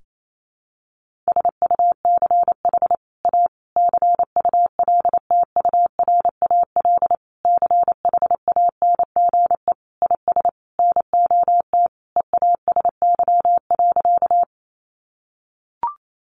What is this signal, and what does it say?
Morse code, heard as a keyed tone, about 28 wpm, 700 Hz. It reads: SUCH A CULTURAL CHANGE IS NOT EASY.
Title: Sentences from Top 1500 Words 28wpm